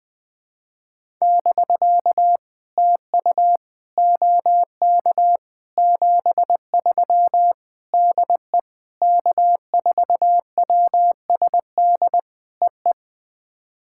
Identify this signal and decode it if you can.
Morse code, heard as a keyed tone, about 20 wpm, 700 Hz. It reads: <BK> TU OK 73 DE K4WSD EE
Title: POTA QSO Practice - 20wpm